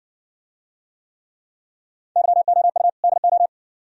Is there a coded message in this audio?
Yes: Morse code from 2.2 to 3.5 s, heard as a keyed tone, about 55 wpm, 690 Hz. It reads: YOU DO